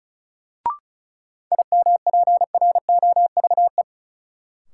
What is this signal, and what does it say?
Morse code, heard as a keyed tone, about 36 wpm, 700 Hz. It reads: IMPROVE
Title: Sentences from Top 800 Words - New Word Drill - 35wpm